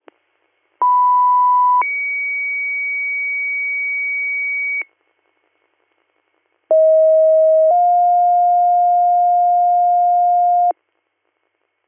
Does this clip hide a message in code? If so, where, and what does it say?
Yes: two-tone page from 0.8 to 10.7 s, heard as a one-second tone, then a three-second tone, about 990 then 2250 Hz, about 640 then 710 Hz.